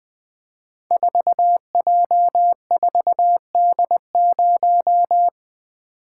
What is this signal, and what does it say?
Morse code, heard as a keyed tone, about 20 wpm, 700 Hz. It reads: 4J4D0
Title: Code Groups - 5 Letter:Numbers - 20wpm